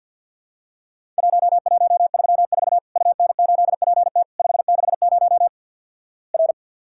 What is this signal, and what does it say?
Morse code, heard as a keyed tone, about 51 wpm, 700 Hz. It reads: 1134 UN8PT 560 R
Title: Worked All Europe DX Contest - QTC Exchange - 50wpm